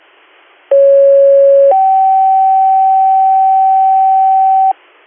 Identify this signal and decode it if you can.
Two-tone page, heard as a one-second tone, then a three-second tone, about 550 then 770 Hz, with noise. It is heard from 0.7 to 4.7 s.